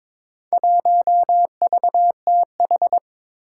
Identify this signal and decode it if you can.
Morse code, heard as a keyed tone, about 22 wpm, 700 Hz. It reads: 1VTH
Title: Four Challenging Letter-Number Combinations 22wpm